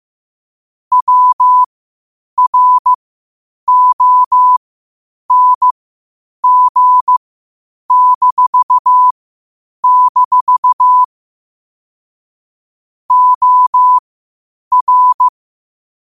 Morse code, heard as a keyed tone, about 15 wpm, spaced as for 8 wpm, 1000 Hz. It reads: WRONG-- OR
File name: qotdfiles / 8wpm